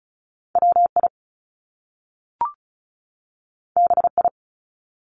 Morse code, heard as a keyed tone, about 36 wpm, 700 Hz. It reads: WI BI
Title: Top Two Letter Combinations in Words 35wpm